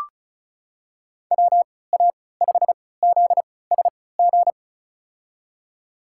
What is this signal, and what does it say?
Morse code, heard as a keyed tone, about 36 wpm, spaced as for 20 wpm, 700 Hz. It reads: WA5ZSG
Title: USA Call Signs - 6 Characters - 3x Character Spacing - 35wpm